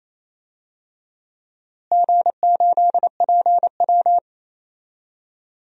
Morse code, heard as a keyed tone, about 28 wpm, 700 Hz. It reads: G8PW
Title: DXCC-100 Call Signs - 4 Characters - Repeated 3x - 28wpm